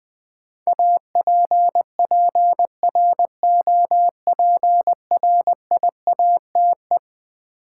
Morse code, heard as a keyed tone, about 20 wpm, 700 Hz. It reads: APPROPRIATE
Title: Sentences from Top 900 Words - New Word Drill - 20wpm